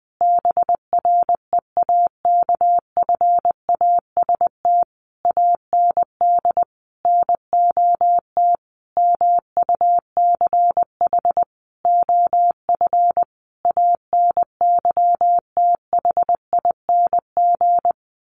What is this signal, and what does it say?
Morse code, heard as a keyed tone, about 19 wpm, 700 Hz. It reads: BREAKFAST AND NOT MUCH OF ANYTHING
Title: Sentences from Top 1500 Words 20wpm